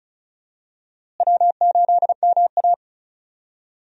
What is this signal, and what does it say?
Morse code, heard as a keyed tone, about 36 wpm, 700 Hz. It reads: W8MA
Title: USA Call Signs - 4 Characters - Repeated 2x - 35wpm